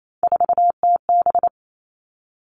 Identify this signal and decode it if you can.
Morse code, heard as a keyed tone, about 28 wpm, 690 Hz. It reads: 4TB
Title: Three Challenging Letter-Number Combinations 28wpm